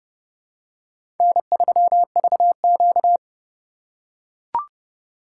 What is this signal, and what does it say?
Morse code, heard as a keyed tone, about 29 wpm, 700 Hz. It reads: N3VQ
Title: USA Call Signs - 4 Characters - 3x Character Spacing - 30wpm